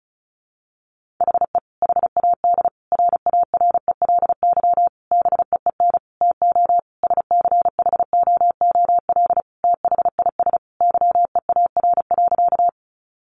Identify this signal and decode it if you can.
Morse code, heard as a keyed tone, about 34 wpm, 700 Hz. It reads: HE HAD RARELY BEEN TO SCHOOL THIS YEAR.
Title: Sentences from Top 1400 Words 35wpm